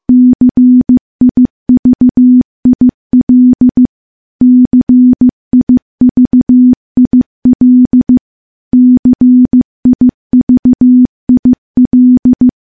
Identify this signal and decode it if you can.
Morse code, heard as a keyed tone, about 15 wpm, 250 Hz. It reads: CIVIL CIVIL CIVIL
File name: morse / morse-training-15WPM